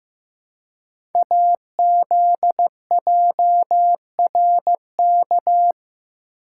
Morse code, heard as a keyed tone, about 15 wpm, 700 Hz. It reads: AZJRK
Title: Code Groups - 5 Letter:Numbers - 15wpm